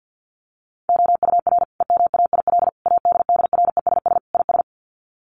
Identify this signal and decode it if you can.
Morse code, heard as a keyed tone, about 50 wpm, 700 Hz. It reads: OUR EMAIL ADDRESS IS
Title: Sets of 5 Words Encore2 50wpm